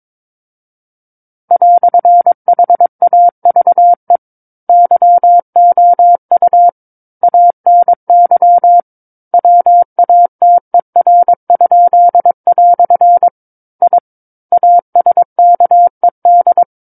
Morse code, heard as a keyed tone, about 22 wpm, 700 Hz. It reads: "HAVE YOU ANY WATER?" I ASKED
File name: Book One / 13-Chapter 13 - 22 WPM